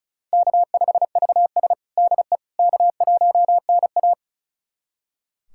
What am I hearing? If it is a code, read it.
Morse code, heard as a keyed tone, about 36 wpm, 700 Hz. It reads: K5VS DE K1NA
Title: USA Callsign DE Callsign - 4 Characters - 35wpm